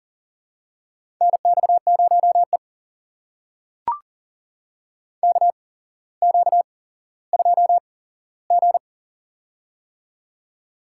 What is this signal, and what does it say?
Morse code, heard as a keyed tone, about 37 wpm, spaced as for 17 wpm, 700 Hz. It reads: NX0E KQ2G
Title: USA Call Signs - 4 Characters - 8x Character Spacing - 40wpm